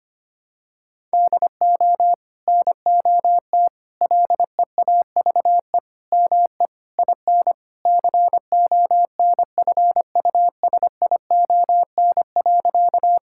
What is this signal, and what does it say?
Morse code, heard as a keyed tone, about 25 wpm, 700 Hz. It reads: DO NOT LEAVE ME IN CONFUSION.
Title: Sentences from Top 1600 Words 25wpm